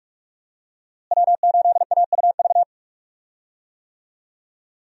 Morse code, heard as a keyed tone, about 41 wpm, 700 Hz. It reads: W8AUV
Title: USA Call Signs - 5 Characters - Repeated 2x - 45wpm